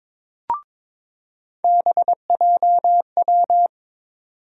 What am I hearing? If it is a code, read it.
Morse code, heard as a keyed tone, about 22 wpm, 700 Hz. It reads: BJW
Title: USA Call Sign Postfixes - 3 Letters 22wpm